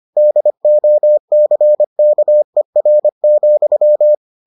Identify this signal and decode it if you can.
Morse code, heard as a keyed tone, about 25 wpm, 600 Hz. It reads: DOCKER,